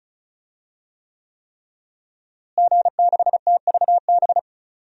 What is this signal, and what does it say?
Morse code, heard as a keyed tone, about 34 wpm, 700 Hz. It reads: G6TVB
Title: DXCC-100 Call Signs - 5 Characters - Repeated 3x - 35wpm